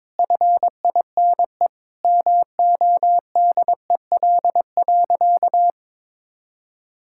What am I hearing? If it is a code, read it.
Morse code, heard as a keyed tone, about 22 wpm, 700 Hz. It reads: FINE MODEL.
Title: Sentences from Top 1000 Words - Review Part 5 - 22wpm